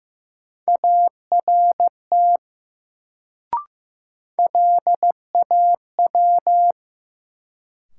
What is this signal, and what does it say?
Morse code, heard as a keyed tone, about 15 wpm, 700 Hz. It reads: ART LAW
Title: Sentences from Top 600 Words - New Word Drill - 15wpm